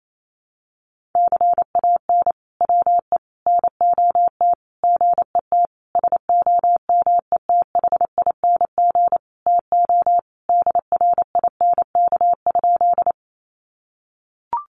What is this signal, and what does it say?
Morse code, heard as a keyed tone, about 28 wpm, 700 Hz. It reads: CAN WE NOT GET SOMETHING TO DRINK?
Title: Sentences from Top 1500 Words 28wpm